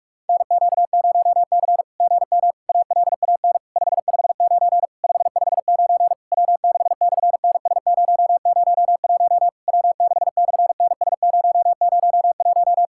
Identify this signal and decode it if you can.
Morse code, heard as a keyed tone, about 42 wpm, 700 Hz. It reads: NQ0C GM ALAN 559 559 W6/NS001 W6/NS001
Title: SOTA QSO Practice 45wpm